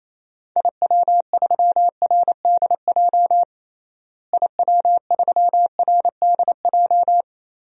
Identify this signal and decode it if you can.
Morse code, heard as a keyed tone, about 27 wpm, 700 Hz. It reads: IW3RDJ IW3RDJ
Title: DXCC-100 Call Signs - 6 Characters - Repeated 2x - 28wpm